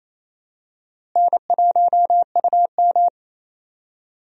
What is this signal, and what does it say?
Morse code, heard as a keyed tone, about 28 wpm, 700 Hz. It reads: N1UM
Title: USA Call Signs - 4 Characters - Repeated 2x - 28wpm